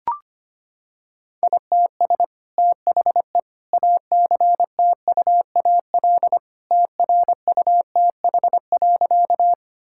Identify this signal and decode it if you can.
Morse code, heard as a keyed tone, about 25 wpm, 700 Hz. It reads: ITS THE ACTUAL TRUTH.
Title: Sentences from Top 1000 Words - Review Part 5 - 25wpm